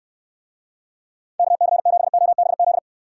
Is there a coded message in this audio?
Yes: Morse code from 1.4 to 2.8 s, heard as a keyed tone, about 69 wpm, 700 Hz. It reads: KC7OBZ